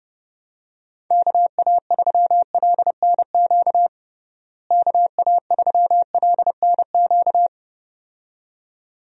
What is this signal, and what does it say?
Morse code, heard as a keyed tone, about 32 wpm, 700 Hz. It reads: KA3LNQ KA3LNQ
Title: USA Call Signs - 6 Characters - Repeated 2x - 30wpm